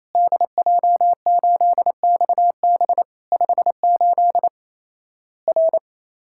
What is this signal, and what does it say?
Morse code, heard as a keyed tone, about 28 wpm, 700 Hz. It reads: DJ8XB 58 R
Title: Worked All Europe DX Contest - QTC Exchange - 28wpm